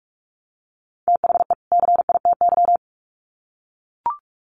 Morse code, heard as a keyed tone, about 44 wpm, 700 Hz. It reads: THE CITY
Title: Sets of 2 Words 45wpm